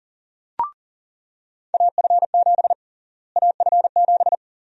Morse code, heard as a keyed tone, about 40 wpm, 700 Hz. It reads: AF7 AF7
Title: USA Call Sign Prefixes - Repeated 2x - 40wpm